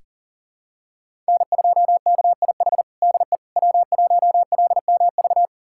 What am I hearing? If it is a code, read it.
Morse code, heard as a keyed tone, about 40 wpm, 700 Hz. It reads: N2KIH DE W1LMV
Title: USA Callsign DE Callsign - 5 Characters - 40wpm